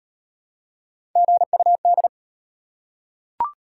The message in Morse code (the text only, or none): GUD